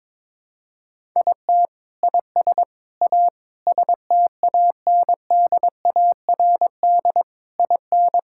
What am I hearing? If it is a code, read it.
Morse code, heard as a keyed tone, about 22 wpm, 700 Hz. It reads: IT IS A STANDARD IN
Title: Sentences from Top 1000 Words - Review Part 6 - 22wpm